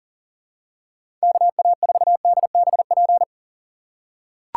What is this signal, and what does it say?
Morse code, heard as a keyed tone, about 40 wpm, 700 Hz. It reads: KA4DBP
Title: USA Call Signs - 6x Character Spacing - 40wpm